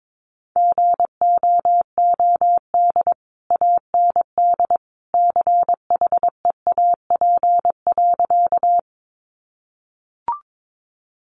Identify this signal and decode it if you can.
Morse code, heard as a keyed tone, about 22 wpm, 700 Hz. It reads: GOOD AND CHEAP.
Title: Sentences from Top 1200 Words 22wpm